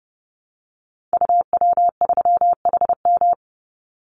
Morse code, heard as a keyed tone, about 30 wpm, 700 Hz. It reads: UW3HM
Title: Call Signs - 30wpm